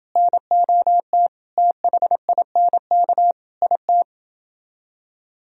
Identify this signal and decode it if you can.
Morse code, heard as a keyed tone, about 29 wpm, 700 Hz. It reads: NOT THINK IT